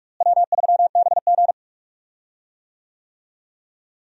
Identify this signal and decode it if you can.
Morse code, heard as a keyed tone, about 43 wpm, 700 Hz. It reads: W3BG
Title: USA Call Signs - 4 Characters - Repeated 2x - 45wpm